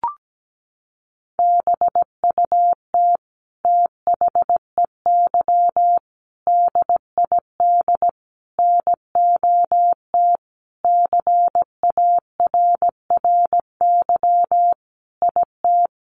Morse code, heard as a keyed tone, about 17 wpm, 700 Hz. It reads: BUT THEY DID NOT CARRY IT
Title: Sentences from Top 1400 Words 17wpm